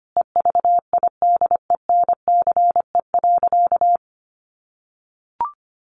Morse code, heard as a keyed tone, about 25 wpm, 700 Hz. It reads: EVIDENCE.
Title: Sentences from Top 1300 Words 25wpm